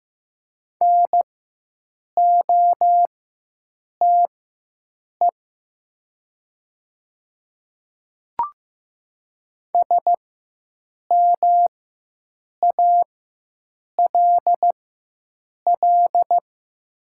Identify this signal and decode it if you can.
Morse code, heard as a keyed tone, about 15 wpm, spaced as for 7 wpm, 700 Hz. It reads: NOTE SMALL
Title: 4x Character Spacing - Top 500 Words - 15wpm